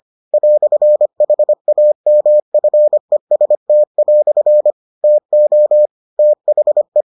Morse code, heard as a keyed tone, about 25 wpm, 600 Hz. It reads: "HAMFEST" TO THE